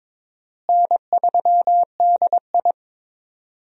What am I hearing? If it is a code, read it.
Morse code, heard as a keyed tone, about 22 wpm, 700 Hz. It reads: N3DI
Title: USA Call Signs - 4 Characters - Repeated 3x - 22wpm